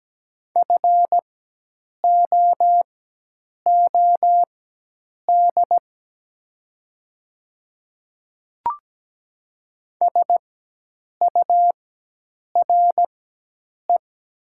Morse code, heard as a keyed tone, about 17 wpm, spaced as for 8 wpm, 700 Hz. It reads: FOOD SURE
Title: 4x Character Spacing - Top 300 Words - 17wpm